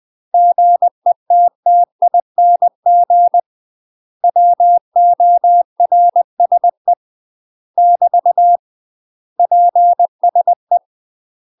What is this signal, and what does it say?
Morse code, heard as a keyed tone, about 20 wpm, 700 Hz. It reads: GETTING WORSE <BT> PSE
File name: UK MASTER QSOs TRAINING / 20WPM / UK MASTER QSOs 0065